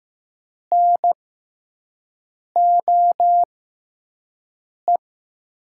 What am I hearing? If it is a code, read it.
Morse code, heard as a keyed tone, about 15 wpm, spaced as for 5 wpm, 700 Hz. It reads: NOE
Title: USA Call Signs - 5 Characters - 6x Character Spacing - 15wpm